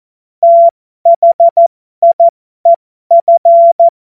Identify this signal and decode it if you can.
Morse code, heard as a keyed tone, about 14 wpm, spaced as for 12 wpm, 680 Hz. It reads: THIEF
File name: Morsekurs / Witze